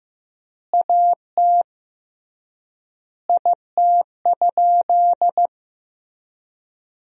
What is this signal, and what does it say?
Morse code, heard as a keyed tone, about 15 wpm, 700 Hz. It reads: AT IT?
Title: 3x Word-Spacing - Sentences from Top 100 Words - 15wpm